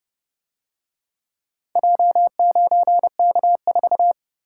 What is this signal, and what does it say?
Morse code, heard as a keyed tone, about 28 wpm, 700 Hz. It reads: J9K4